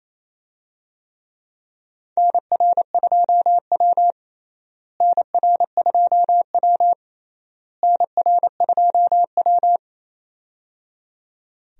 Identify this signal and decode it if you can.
Morse code, heard as a keyed tone, about 28 wpm, 700 Hz. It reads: NR2W NR2W NR2W
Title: USA Call Signs - 4 Characters - Repeated 3x - 28wpm